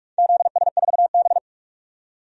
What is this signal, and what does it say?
Morse code, heard as a keyed tone, about 44 wpm, 700 Hz. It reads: ZS4B
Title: DXCC-100 Call Signs - 4 Characters - 45wpm